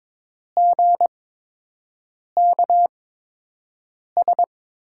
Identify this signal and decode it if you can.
Morse code, heard as a keyed tone, about 22 wpm, spaced as for 6 wpm, 700 Hz. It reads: GKS
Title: USA Call Signs - 5 Characters - 8x Character Spacing - 22wpm